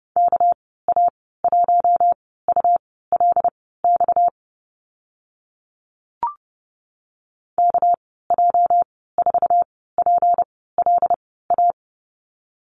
Morse code, heard as a keyed tone, about 30 wpm, spaced as for 17 wpm, 700 Hz. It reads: KA1ULX KJ4PLA
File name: Lesson 37g - Letter x - Call Signs 30wpm